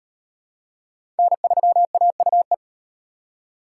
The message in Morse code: N3AUE

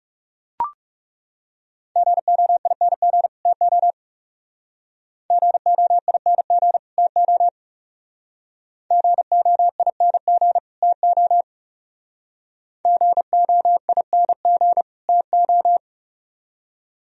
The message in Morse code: GOING TO GOING TO GOING TO GOING TO

